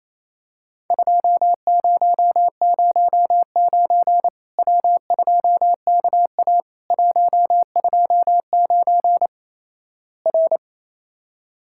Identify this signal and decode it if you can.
Morse code, heard as a keyed tone, about 28 wpm, 700 Hz. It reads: 2009 W2KA 129 R
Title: Worked All Europe DX Contest - QTC Exchange - 28wpm